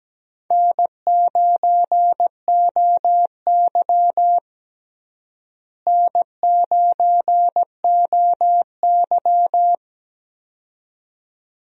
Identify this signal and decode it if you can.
Morse code, heard as a keyed tone, about 17 wpm, 700 Hz. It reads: N9OY N9OY
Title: USA Call Signs - 4 Characters - Repeated 2x - 17wpm